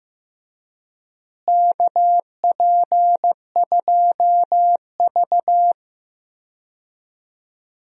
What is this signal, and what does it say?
Morse code, heard as a keyed tone, about 15 wpm, 700 Hz. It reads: KP2V
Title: DXCC-100 Call Signs - 4 Characters - Repeated 3x - 15wpm